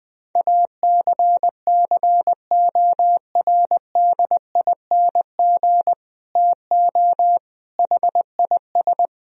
Morse code, heard as a keyed tone, about 20 wpm, 700 Hz. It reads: ACCORDING TO HIS
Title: Sentences from Top 1000 Words - Review Part 4 - 20wpm